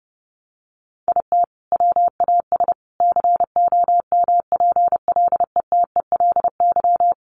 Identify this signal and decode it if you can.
Morse code, heard as a keyed tone, about 30 wpm, 700 Hz. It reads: IT WAS COMPLETELY
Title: Sentences from Top 1100 Words 30wpm